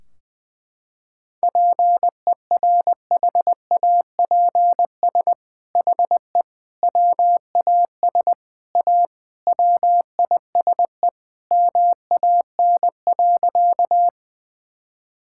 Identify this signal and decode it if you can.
Morse code, heard as a keyed tone, about 21 wpm, 700 Hz. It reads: PERHAPS HE WAS A WISE MAN.
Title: Sentences from Top 1600 Words 20wpm